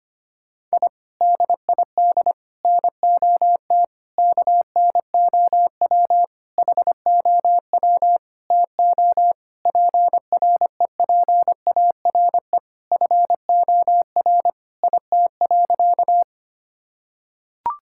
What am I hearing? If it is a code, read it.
Morse code, heard as a keyed tone, about 25 wpm, 700 Hz. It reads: I DID NOT KNOW HOW TO PREPARE FOR IT.
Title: Sentences from Top 1000 Words - Review Part 6 - 25wpm